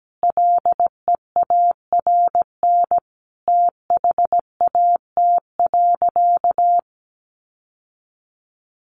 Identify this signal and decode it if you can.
Morse code, heard as a keyed tone, about 17 wpm, 700 Hz. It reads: LEARN THAT.
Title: Sentences from Top 1200 Words 17wpm